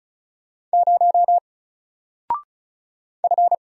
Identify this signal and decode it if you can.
Morse code, heard as a keyed tone, about 36 wpm, 700 Hz. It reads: 0 F
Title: Single Character (Letter:Number:Punctuation:Prosign) 35wpm